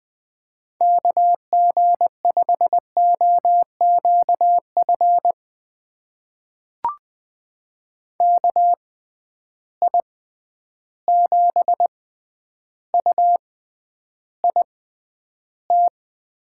Morse code, heard as a keyed tone, about 20 wpm, spaced as for 10 wpm, 700 Hz. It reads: KG5OQF KI7UIT
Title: USA Call Signs - 6 Characters - 6x Character Spacing - 20wpm